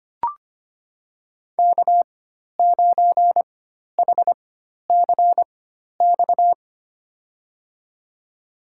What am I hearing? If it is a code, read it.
Morse code, heard as a keyed tone, about 25 wpm, spaced as for 12 wpm, 700 Hz. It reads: K9HCX
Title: USA Call Signs - 5 Characters - 4x Character Spacing - 25wpm